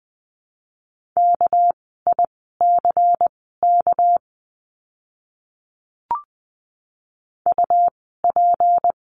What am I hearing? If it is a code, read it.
Morse code, heard as a keyed tone, about 19 wpm, spaced as for 14 wpm, 700 Hz. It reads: KICK UP
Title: 199 Lesson 32f - Letter k - Words 20wpm